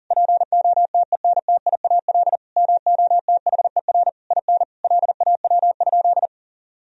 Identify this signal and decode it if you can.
Morse code, heard as a keyed tone, about 40 wpm, 700 Hz. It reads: POTENTIAL MOTHER IN LAW?
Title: Sentences from Top 1000 Words - Review Part 4 - 40wpm